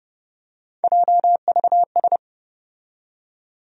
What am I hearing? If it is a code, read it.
Morse code, heard as a keyed tone, about 30 wpm, 700 Hz. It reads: JVS
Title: USA Call Sign Postfixes - 3 Letters 30wpm